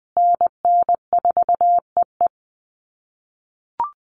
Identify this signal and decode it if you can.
Morse code, heard as a keyed tone, about 20 wpm, 700 Hz. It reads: NN4EE
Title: USA Call Signs Repeated 2x 20wpm